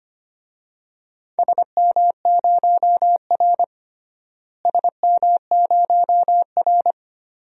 Morse code, heard as a keyed tone, about 25 wpm, 700 Hz. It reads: SM0R SM0R
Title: DXCC-100 Call Signs - 4 Characters - Repeated 2x - 25wpm